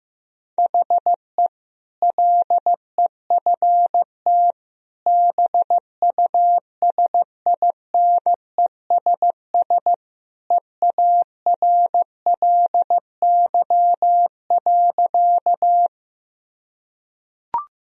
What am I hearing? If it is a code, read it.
Morse code, heard as a keyed tone, about 16 wpm, 700 Hz. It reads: HE LEFT BUSINESS EARLY.
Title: Sentences from Top 1000 Words - Review Part 2 - 15wpm